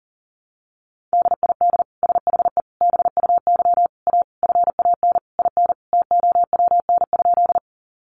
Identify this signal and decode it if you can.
Morse code, heard as a keyed tone, about 40 wpm, 700 Hz. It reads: DID SHE BUY A FAN IN TOWN?
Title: Sentences from Top 1300 Words 40wpm